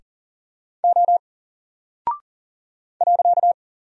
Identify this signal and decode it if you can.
Morse code, heard as a keyed tone, about 40 wpm, 700 Hz. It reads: O .
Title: Single Character (Letter:Number:Punctuation:Prosign) 40wpm